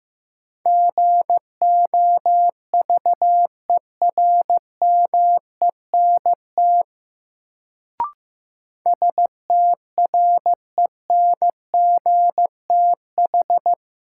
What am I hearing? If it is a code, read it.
Morse code, heard as a keyed tone, about 15 wpm, 700 Hz. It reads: GOVERMENT STRENGTH
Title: Sentences from Top 1300 Words - New Word Drill Alt - 15wpm